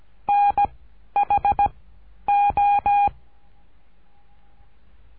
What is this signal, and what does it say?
Morse code, heard as a keyed tone, about 17 wpm, 810 Hz. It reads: N H O
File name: cursos / CURSO DE TELEGRAFIA / Telegrafia - Trilha 13